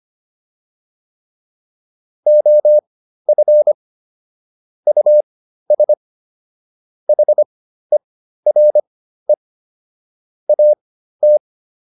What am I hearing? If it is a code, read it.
Morse code, heard as a keyed tone, about 25 wpm, spaced as for 13 wpm, 600 Hz. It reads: OF US HERE AT